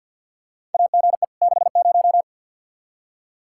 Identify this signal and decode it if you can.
Morse code, heard as a keyed tone, about 50 wpm, 700 Hz. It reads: AGE 60
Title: Practice QSO - Exchange 3 - More Info - 50wpm